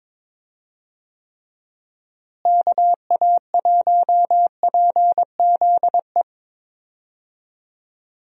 Morse code, heard as a keyed tone, about 21 wpm, 700 Hz. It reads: KA1PZE
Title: USA Call Signs - 6 Characters - Repeated 3x - 22wpm